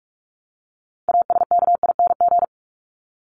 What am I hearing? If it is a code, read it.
Morse code, heard as a keyed tone, about 44 wpm, 700 Hz. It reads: ASKING
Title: ING Suffix - Top 100 Words with Base Words 45wpm